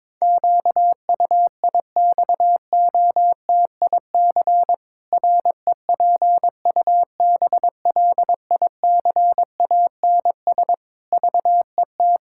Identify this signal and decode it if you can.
Morse code, heard as a keyed tone, about 22 wpm, 700 Hz. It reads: QUIXOTIC REPUBLICANS VET
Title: Pangram Challenge v2 - 22wpm